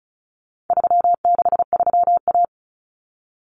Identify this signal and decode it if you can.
Morse code, heard as a keyed tone, about 34 wpm, 700 Hz. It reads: 363A
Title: Common Cut Numbers - 4-Digit 35wpm